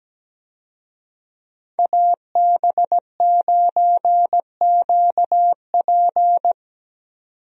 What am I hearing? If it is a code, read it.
Morse code, heard as a keyed tone, about 17 wpm, 700 Hz. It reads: AB9QP
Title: USA Call Signs - 5 Characters - 17wpm